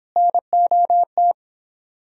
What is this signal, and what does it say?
Morse code, heard as a keyed tone, about 29 wpm, 700 Hz. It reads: NOT